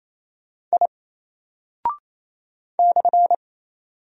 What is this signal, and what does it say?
Morse code, heard as a keyed tone, about 28 wpm, 700 Hz. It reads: I /
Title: Single Character (Letter:Number:Punctuation:Prosign) Alt 28wpm